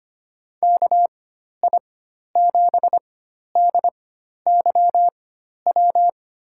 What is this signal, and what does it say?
Morse code, heard as a keyed tone, about 25 wpm, spaced as for 12 wpm, 700 Hz. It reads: KI7DYW
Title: USA Call Signs - 4x Character Spacing - 25wpm